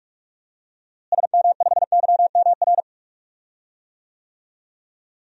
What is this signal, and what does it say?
Morse code, heard as a keyed tone, about 45 wpm, 700 Hz. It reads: SM5YMR
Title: DXCC-100 Call Signs - 6 Characters - Repeated 3x - 45wpm